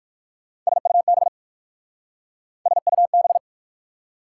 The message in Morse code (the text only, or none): SUB SUB